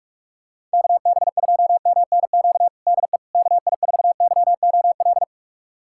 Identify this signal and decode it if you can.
Morse code, heard as a keyed tone, about 42 wpm, 700 Hz. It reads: KB2MNQ DE KI4YOL
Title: USA Callsign DE Callsign - 6 Characters - 45wpm